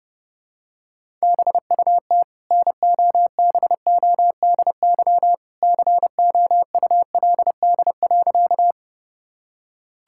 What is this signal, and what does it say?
Morse code, heard as a keyed tone, about 32 wpm, 700 Hz. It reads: BUT NOBODY COULD.
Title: Sentences from Top 1600 Words 30wpm